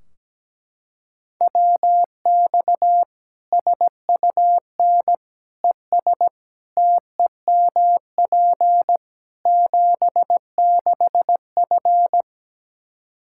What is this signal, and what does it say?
Morse code, heard as a keyed tone, about 17 wpm, 700 Hz. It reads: WX SUN ES TEMP 76F
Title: K1USN SST QSO Practice 17wpm